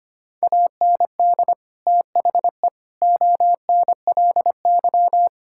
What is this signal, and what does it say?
Morse code, heard as a keyed tone, about 25 wpm, 700 Hz. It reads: AND THE ONLY
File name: practice / 25wpm / Sentences with 9-Word Phrases - 25wpm